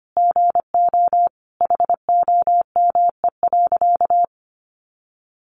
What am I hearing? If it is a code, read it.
Morse code, heard as a keyed tone, about 25 wpm, 700 Hz. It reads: GO HOME.
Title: A Sound Of Thunder 25wpm